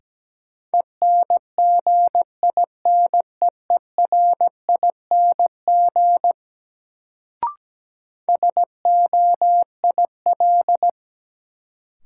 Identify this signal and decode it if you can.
Morse code, heard as a keyed tone, about 17 wpm, 700 Hz. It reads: ENGINEERING SOIL
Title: Sentences from Top 1500 Words - New Word Drill Alt - 17wpm